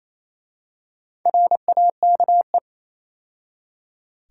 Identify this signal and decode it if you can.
Morse code, heard as a keyed tone, about 27 wpm, 700 Hz. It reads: RAKE
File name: Easy Words - Speed-Racing with Hints - 22wpm